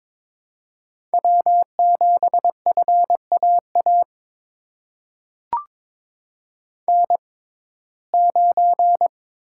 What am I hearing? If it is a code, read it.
Morse code, heard as a keyed tone, about 22 wpm, spaced as for 16 wpm, 700 Hz. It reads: W7FAA N9
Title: USA Call Signs - 5 Characters - 6x Character Spacing - 22wpm